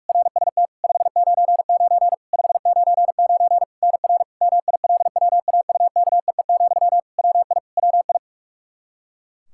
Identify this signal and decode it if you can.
Morse code, heard as a keyed tone, about 42 wpm, 700 Hz. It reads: RST 599 599 NR MILWAUKEE, WI WI
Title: Practice QSO - Exchange 1 - Beginning - 45wpm